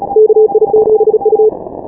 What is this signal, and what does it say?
Morse code, heard as a keyed tone, about 36 wpm, 430 Hz. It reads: KI7U
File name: final / W1UJ-168-140-MorseRunner